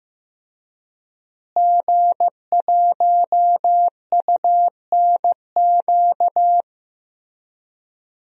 Morse code, heard as a keyed tone, about 15 wpm, 700 Hz. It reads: G1UNQ
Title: DXCC-100 Call Signs - 5 Characters - Repeated 3x - 15wpm